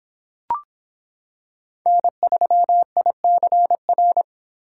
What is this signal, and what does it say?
Morse code, heard as a keyed tone, about 26 wpm, 700 Hz. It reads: N3ICR